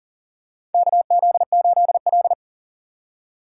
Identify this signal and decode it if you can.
Morse code, heard as a keyed tone, about 40 wpm, 700 Hz. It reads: KZ8L
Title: USA Call Signs - 4x Character Spacing - 40wpm